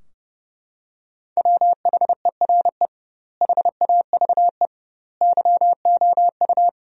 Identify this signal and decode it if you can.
Morse code, heard as a keyed tone, about 30 wpm, 700 Hz. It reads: WHERE HAVE YOU
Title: 2x Word-Spacing - Sentences from Top 300 Words - 30wpm